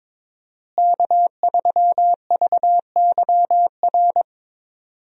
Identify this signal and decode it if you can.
Morse code, heard as a keyed tone, about 22 wpm, 690 Hz. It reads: K3VYR
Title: USA Call Signs - 5 Characters - Repeated 3x - 22wpm